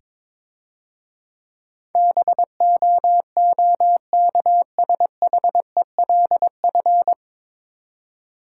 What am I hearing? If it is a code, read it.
Morse code, heard as a keyed tone, about 22 wpm, 700 Hz. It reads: BOOKSHELF
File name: Easy Words - Speed-Racing with Hints - 22wpm